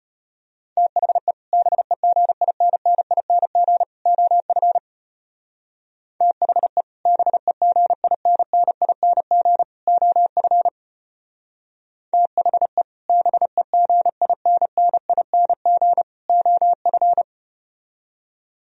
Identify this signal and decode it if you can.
Morse code, heard as a keyed tone, about 33 wpm, 700 Hz. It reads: THE BEGINNING OF THE BEGINNING OF THE BEGINNING OF